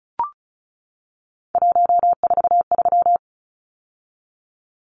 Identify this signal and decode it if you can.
Morse code, heard as a keyed tone, about 34 wpm, 700 Hz. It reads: 143
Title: Numbers - 3-Digit 35wpm